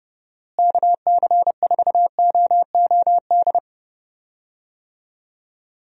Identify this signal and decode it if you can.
Morse code, heard as a keyed tone, about 31 wpm, 700 Hz. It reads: KC4OOD
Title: USA Call Signs - 6 Characters - Repeated 3x - 30wpm